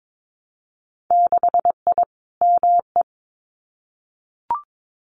Morse code, heard as a keyed tone, about 22 wpm, 700 Hz. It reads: 6I ME
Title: Winter Field Day Exchanges Encore 22wpm